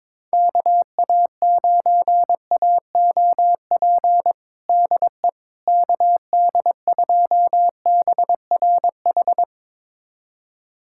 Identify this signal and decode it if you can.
Morse code, heard as a keyed tone, about 22 wpm, 700 Hz. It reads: KA9AOP DE KD2BRH
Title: USA Callsign DE Callsign - 6 Characters - 22wpm